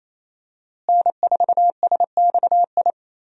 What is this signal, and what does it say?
Morse code, heard as a keyed tone, about 28 wpm, 700 Hz. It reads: N4SXI
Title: USA Call Signs - 5 Characters - Repeated 3x - 28wpm